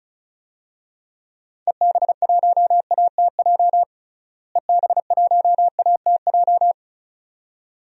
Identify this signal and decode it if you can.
Morse code, heard as a keyed tone, about 34 wpm, 700 Hz. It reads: EB1ATJ EB1ATJ
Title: DXCC-100 Call Signs - 6 Characters - Repeated 2x - 35wpm